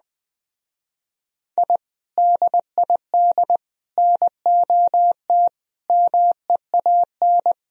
Morse code, heard as a keyed tone, about 20 wpm, 700 Hz. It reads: I DID NOT MEAN